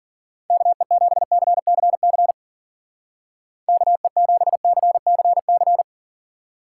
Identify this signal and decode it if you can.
Morse code, heard as a keyed tone, about 52 wpm, 700 Hz. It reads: KE7CCC KE7CCC